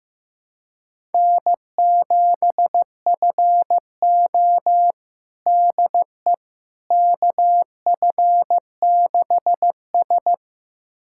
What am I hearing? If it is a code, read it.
Morse code, heard as a keyed tone, about 15 wpm, 700 Hz. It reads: N7FO DE KF6S
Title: USA Callsign DE Callsign - 4 Characters - 15wpm